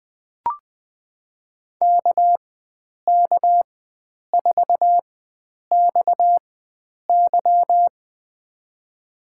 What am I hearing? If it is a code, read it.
Morse code, heard as a keyed tone, about 20 wpm, spaced as for 9 wpm, 700 Hz. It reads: KK4XY
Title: USA Call Signs - 5 Characters - 4x Character Spacing - 20wpm